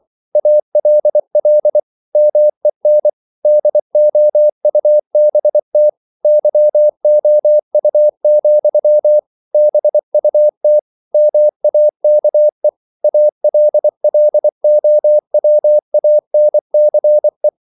Morse code, heard as a keyed tone, about 24 wpm, 600 Hz. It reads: ALL MEN DOUBT YOU, BUT MAKE ALLOWANCE
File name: CW / Practice / If24